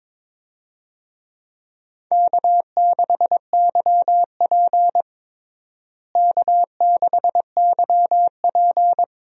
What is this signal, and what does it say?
Morse code, heard as a keyed tone, about 22 wpm, 700 Hz. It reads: K6YP K6YP
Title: USA Call Signs - 4 Characters - Repeated 2x - 22wpm